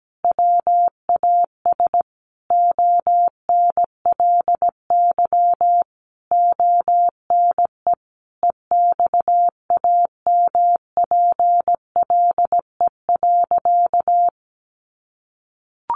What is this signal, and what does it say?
Morse code, heard as a keyed tone, about 17 wpm, 700 Hz. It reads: WAS ONLY ONE EXAMPLE.
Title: Sentences from Top 1500 Words 17wpm